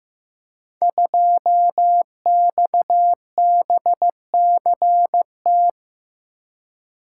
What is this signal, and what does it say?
Morse code, heard as a keyed tone, about 15 wpm, 700 Hz. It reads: 2XBCT
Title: Code Groups - 5 Letter:Numbers - 15wpm